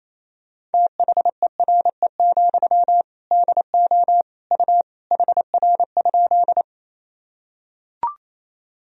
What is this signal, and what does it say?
Morse code, heard as a keyed tone, about 28 wpm, 700 Hz. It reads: THERE, DO U HR?
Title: Sentences from Top 300 Words and Abbreviations 28wpm